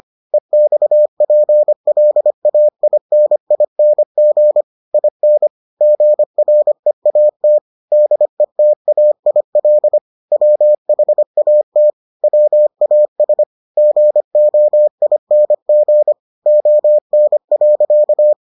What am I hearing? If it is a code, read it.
Morse code, heard as a keyed tone, about 25 wpm, 600 Hz. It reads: EXPLAINING IN GREAT DETAIL WHAT WAS GOING ON.